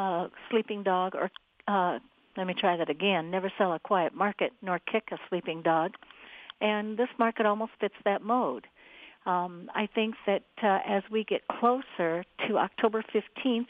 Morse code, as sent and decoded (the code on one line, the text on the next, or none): none